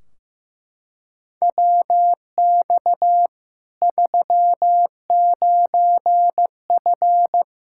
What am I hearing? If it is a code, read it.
Morse code, heard as a keyed tone, about 15 wpm, 700 Hz. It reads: WX 39F
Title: Practice QSO - Exchange 2 - Info - 15wpm